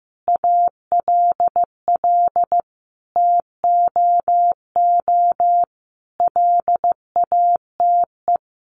Morse code, heard as a keyed tone, about 15 wpm, 700 Hz. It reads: ALL TOO LATE
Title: Sentences from Top 300 Words 15wpm